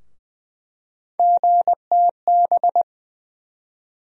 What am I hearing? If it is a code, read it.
Morse code, heard as a keyed tone, about 20 wpm, 700 Hz. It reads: GTB
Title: USA Call Sign Postfixes - 3 Letters 20wpm